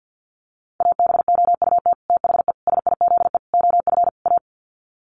Morse code, heard as a keyed tone, about 50 wpm, 700 Hz. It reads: ABOUT THE SIZE OF A
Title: Sets of 5 Words 50wpm